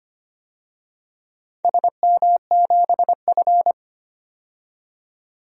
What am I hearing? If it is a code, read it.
Morse code, heard as a keyed tone, about 25 wpm, 700 Hz. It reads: SM7F
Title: DXCC-100 Call Signs - 4 Characters - 25wpm